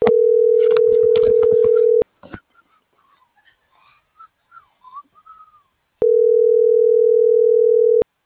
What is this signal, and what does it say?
Ringback tone (North America), 440 + 480 Hz.